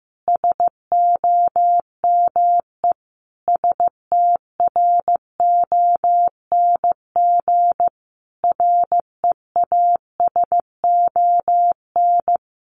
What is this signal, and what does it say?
Morse code, heard as a keyed tone, about 14 wpm, 700 Hz. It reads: SOME STRONG REASON
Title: Sentences from Top 700 Words 15wpm